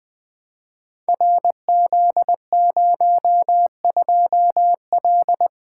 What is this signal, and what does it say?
Morse code, heard as a keyed tone, about 20 wpm, 700 Hz. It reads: RZ02L
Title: Code Groups - 5 Letter:Numbers - 20wpm